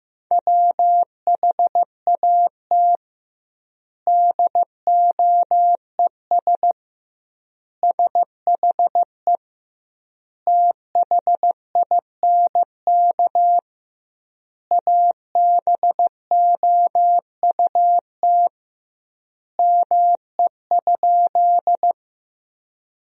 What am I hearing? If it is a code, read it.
Morse code, heard as a keyed tone, about 15 wpm, 700 Hz. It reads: WHAT DOES SHE THINK ABOUT ME?
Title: 2x Word-Spacing - Sentences from Top 200 Words - 15wpm